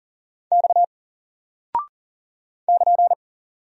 Morse code, heard as a keyed tone, about 40 wpm, 700 Hz. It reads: X <KN>
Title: Single Character (Letter:Number:Punctuation:Prosign) 40wpm